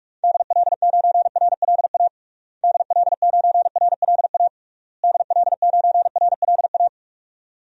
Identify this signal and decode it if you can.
Morse code, heard as a keyed tone, about 46 wpm, 700 Hz. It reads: DL9RLA DL9RLA DL9RLA
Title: DXCC-100 Call Signs - 6 Characters - Repeated 3x - 45wpm